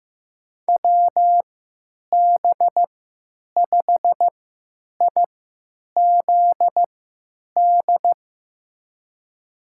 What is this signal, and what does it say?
Morse code, heard as a keyed tone, about 15 wpm, spaced as for 9 wpm, 700 Hz. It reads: WB5IZD
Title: USA Call Signs - 6 Characters - 3x Character Spacing - 15wpm